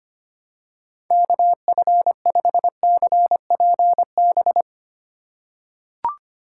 Morse code, heard as a keyed tone, about 25 wpm, 700 Hz. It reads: KF5CPB
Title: USA Call Signs - 6 Characters - 4x Character Spacing - 25wpm